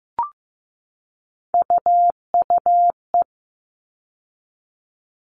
Morse code, heard as a keyed tone, about 15 wpm, 700 Hz. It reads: UUE
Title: All Cut Numbers - 3-Digit 15wpm